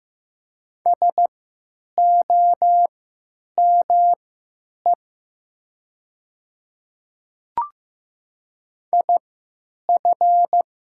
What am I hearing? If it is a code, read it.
Morse code, heard as a keyed tone, about 15 wpm, spaced as for 9 wpm, 700 Hz. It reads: SOME IF
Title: 3x Character Spacing - Top 100 Words - 15wpm